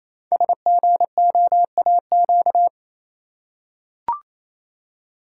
Morse code, heard as a keyed tone, about 29 wpm, 700 Hz. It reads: SGOAQ
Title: Code Groups - 5 Letters - 28wpm